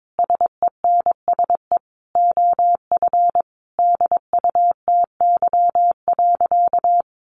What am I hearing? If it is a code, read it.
Morse code, heard as a keyed tone, about 22 wpm, 700 Hz. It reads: SENSE OF DUTY.
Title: Sentences from Top 1500 Words 22wpm